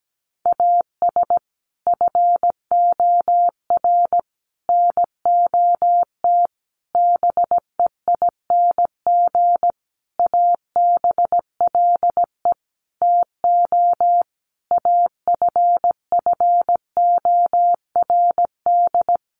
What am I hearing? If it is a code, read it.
Morse code, heard as a keyed tone, about 17 wpm, 700 Hz. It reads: AS FOR NOT BEING ABLE TO AFFORD
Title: Sentences from Top 1100 Words 17wpm